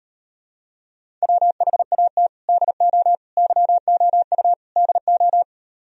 Morse code, heard as a keyed tone, about 38 wpm, 700 Hz. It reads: WHAT DO YOU DO